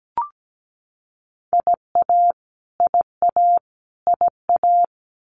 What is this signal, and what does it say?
Morse code, heard as a keyed tone, about 17 wpm, 700 Hz. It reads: IA IA IA
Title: US States Repeated 3x 17wpm